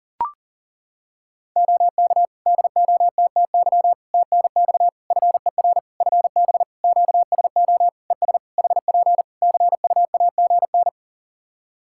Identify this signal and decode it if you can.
Morse code, heard as a keyed tone, about 40 wpm, 700 Hz. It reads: OK DOTTY TNX FER FB QSO ES HP CUAGN
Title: K1USN SST QSO Practice 40wpm